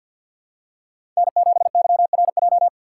Morse code, heard as a keyed tone, about 53 wpm, 700 Hz. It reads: N7YRJ
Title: USA Call Signs - 5 Characters - 4x Character Spacing - 50wpm